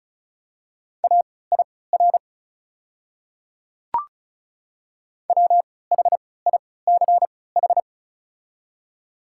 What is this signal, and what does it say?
Morse code, heard as a keyed tone, about 36 wpm, spaced as for 20 wpm, 700 Hz. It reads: AIR WHICH
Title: 3x Character Spacing - Top 200 Words - 35wpm